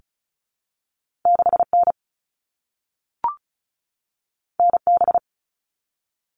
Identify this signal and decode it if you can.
Morse code, heard as a keyed tone, about 36 wpm, 700 Hz. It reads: 6N NB
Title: All Cut Numbers - 2-Digit 35wpm